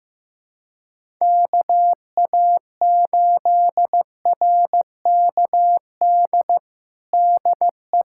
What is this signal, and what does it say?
Morse code, heard as a keyed tone, about 15 wpm, 700 Hz. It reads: KA8RKD DE
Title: USA Callsign DE Callsign - 6 Characters - 15wpm